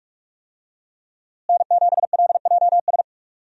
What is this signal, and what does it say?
Morse code, heard as a keyed tone, about 43 wpm, 700 Hz. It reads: N7LJS
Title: USA Call Signs - 5 Characters - Repeated 2x - 45wpm